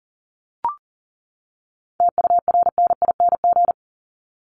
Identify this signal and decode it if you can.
Morse code, heard as a keyed tone, about 40 wpm, 700 Hz. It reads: TURNING
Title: ING Suffix - Top 100 Words with Base Words 40wpm